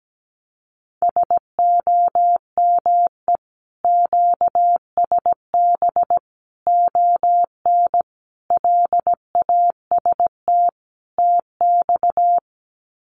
Morse code, heard as a keyed tone, about 17 wpm, 700 Hz. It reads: SOME QSB ON LAST TX
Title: Common QSO Phrases 17wpm